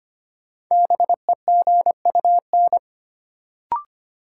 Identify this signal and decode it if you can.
Morse code, heard as a keyed tone, about 25 wpm, 700 Hz. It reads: BEGUN
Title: Sentences from Top 1000 Words - Review Part 6 - New Word Drill - 25wpm